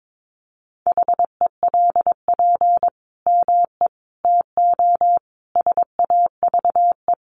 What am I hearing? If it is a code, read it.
Morse code, heard as a keyed tone, about 22 wpm, 700 Hz. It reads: HELP ME TO SAVE
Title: Sentences from Top 1100 Words 22wpm